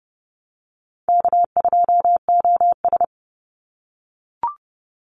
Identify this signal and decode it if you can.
Morse code, heard as a keyed tone, about 30 wpm, 700 Hz. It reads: K2OS
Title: Easy Call Signs - 30wpm